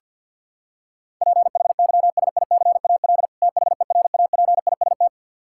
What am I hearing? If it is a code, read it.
Morse code, heard as a keyed tone, about 48 wpm, 700 Hz. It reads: PHYSICAL THERAPIST
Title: Practice QSO - Exchange 3 - More Info - 50wpm